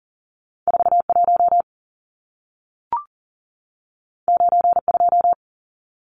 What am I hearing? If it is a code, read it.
Morse code, heard as a keyed tone, about 40 wpm, 700 Hz. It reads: 41 92
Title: Numbers - 2-Digit 40wpm